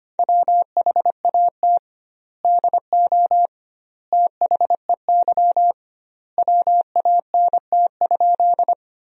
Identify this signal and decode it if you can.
Morse code, heard as a keyed tone, about 25 wpm, 700 Hz. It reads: WHAT DO THEY WANT?
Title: 2x Word-Spacing - Sentences from Top 200 Words - 25wpm